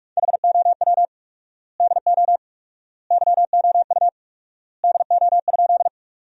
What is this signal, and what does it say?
Morse code, heard as a keyed tone, about 47 wpm, 700 Hz. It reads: HOW DO YOU DO?
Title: 4x Word-Spacing - Sentences from Top 100 Words - 45wpm